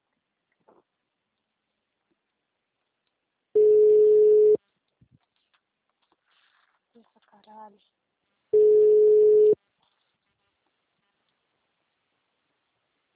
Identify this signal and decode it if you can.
Ringback tone (Europe, 425 Hz), with speech.
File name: out- -2027- - - .35